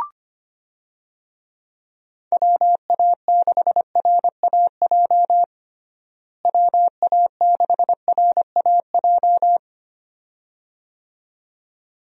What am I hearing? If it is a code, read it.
Morse code, heard as a keyed tone, about 25 wpm, 700 Hz. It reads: WA6RAJ WA6RAJ
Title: USA Call Signs - 6 Characters - Repeated 2x - 25wpm